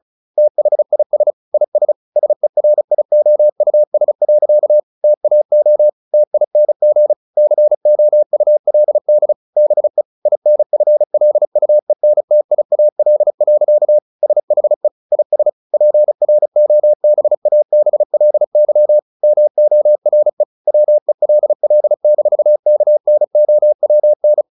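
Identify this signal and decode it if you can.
Morse code, heard as a keyed tone, about 34 wpm, 600 Hz. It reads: THIS IS SERIOUS. TAO TING COULD BE INFLUENTIAL. SHE IS PROBABLY MORE WELL-KNOWN
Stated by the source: Morse practice at 35 wpm from CW med Gnister, sending Ken Follett's novel Never in English.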